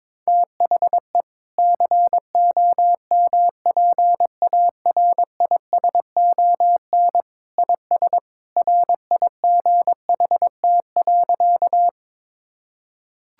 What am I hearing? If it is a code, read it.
Morse code, heard as a keyed tone, about 21 wpm, 700 Hz. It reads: THE COMPARISON IS RIGHT.
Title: Sentences from Top 1600 Words 22wpm